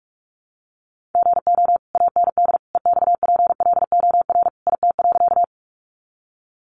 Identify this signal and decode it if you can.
Morse code, heard as a keyed tone, about 44 wpm, 700 Hz. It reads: GO AND EXPLOR IT.
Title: Sentences from Top 1100 Words 45wpm